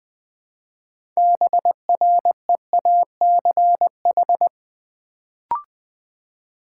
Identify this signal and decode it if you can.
Morse code, heard as a keyed tone, about 20 wpm, 700 Hz. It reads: BREACH
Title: Build A Word - 2 Letters At A Time - 6-Letter Words - 20wpm